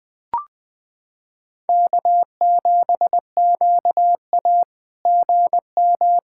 Morse code, heard as a keyed tone, about 20 wpm, 690 Hz. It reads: K7QA GM
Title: SOTA QSO Practice 20wpm